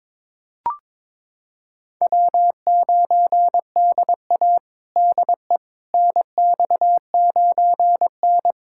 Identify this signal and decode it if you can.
Morse code, heard as a keyed tone, about 22 wpm, 700 Hz. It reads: W9DA DE NX9N
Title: USA Callsign DE Callsign - 4 Characters - 22wpm